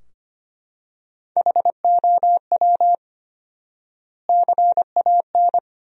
Morse code, heard as a keyed tone, about 25 wpm, 700 Hz. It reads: HOW CAN
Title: 4x Word-Spacing - Sentences from Top 100 Words - 25wpm